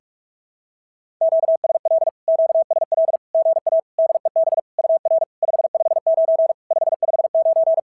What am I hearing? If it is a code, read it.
Morse code, heard as a keyed tone, about 44 wpm, 650 Hz. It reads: QSL QSL GA DEB UR 559 559
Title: SOTA QSO Practice 45wpm